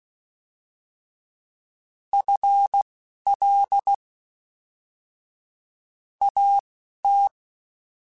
Morse code, heard as a keyed tone, about 16 wpm, spaced as for 12 wpm, 800 Hz. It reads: FL AT
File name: FL / FL mp3s / Fast FL @10